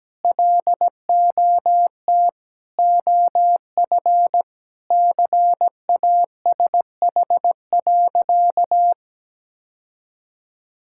Morse code, heard as a keyed tone, about 17 wpm, 700 Hz. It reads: LOT OF CASH.
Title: Sentences from Top 1000 Words - Review Part 5 - 17wpm